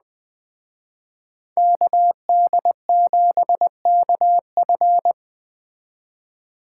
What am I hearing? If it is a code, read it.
Morse code, heard as a keyed tone, about 21 wpm, 700 Hz. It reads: KD7KF
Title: Speed Racing - USA Call Signs 17wpm